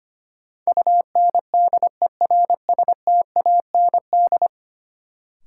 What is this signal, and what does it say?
Morse code, heard as a keyed tone, about 25 wpm, 700 Hz. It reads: UNDERSTAND
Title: Sets of 4 Words - New Word Drill - 25wpm